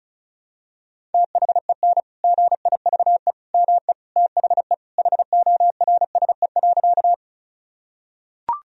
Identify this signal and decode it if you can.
Morse code, heard as a keyed tone, about 36 wpm, 700 Hz. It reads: THEN GIVE ME THE HORSE.
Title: Sentences from Top 1000 Words - Review Part 1 - 35wpm